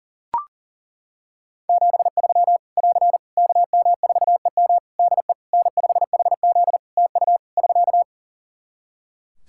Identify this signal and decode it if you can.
Morse code, heard as a keyed tone, about 40 wpm, 700 Hz. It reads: 73 <AR> KM4EM DE N5HZ TU <SK>
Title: Practice QSO - Exchange 4 - Ending - 40wpm